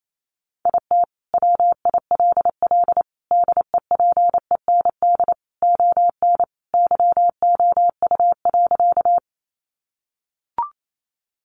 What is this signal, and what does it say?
Morse code, heard as a keyed tone, about 28 wpm, 700 Hz. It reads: IT WILL DEPEND ON YOU.
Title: Sentences from Top 1300 Words 28wpm